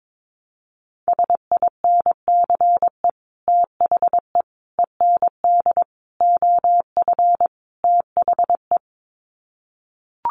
Morse code, heard as a keyed tone, about 22 wpm, 700 Hz. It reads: SINCE THE END OF THE
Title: Sets of 5 Words Encore 22wpm